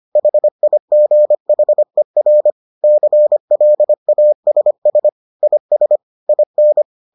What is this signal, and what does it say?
Morse code, heard as a keyed tone, about 25 wpm, 600 Hz. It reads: HIGHER CLASS IS IN